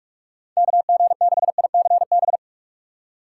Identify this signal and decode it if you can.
Morse code, heard as a keyed tone, about 46 wpm, 700 Hz. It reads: KG6ICB
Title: USA Call Signs - 6 Characters - 8x Character Spacing - 45wpm